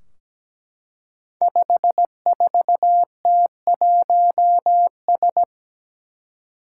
Morse code, heard as a keyed tone, about 17 wpm, 700 Hz. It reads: 54T1S
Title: Five Challenging Letter-Number Combinations 17wpm